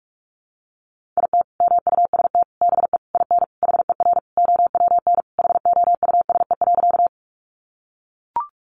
Morse code, heard as a keyed tone, about 44 wpm, 700 Hz. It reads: IT MUST BE IN HER OWN HOUSE.
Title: Sentences from Top 300 Words 45wpm